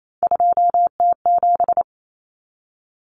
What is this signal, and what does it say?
Morse code, heard as a keyed tone, about 28 wpm, 700 Hz. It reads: 2T7
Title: Common Cut Numbers - 3-Digit 28wpm